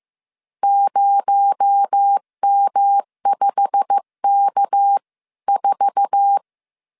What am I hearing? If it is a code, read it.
Morse code, heard as a keyed tone, about 15 wpm, 800 Hz. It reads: 0M5K4
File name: CW / DET / 1 9 1500C S A 14wpm 3222